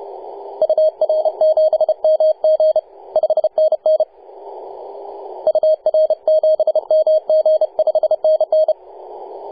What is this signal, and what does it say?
Morse code, heard as a keyed tone, about 32 wpm, 630 Hz, with radio noise. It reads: UR7MG 5NN UR7MG 5NN
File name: posts / 2015-nov-18-0552z-vk9wa-24-894mhz-cw